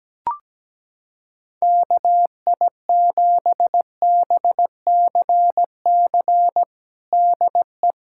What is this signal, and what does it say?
Morse code, heard as a keyed tone, about 17 wpm, 700 Hz. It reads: KI7BCC DE
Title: USA Callsign DE Callsign 17wpm